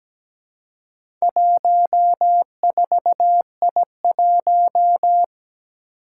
Morse code, heard as a keyed tone, about 17 wpm, 700 Hz. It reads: 14I1
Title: Four Challenging Letter-Number Combinations 17wpm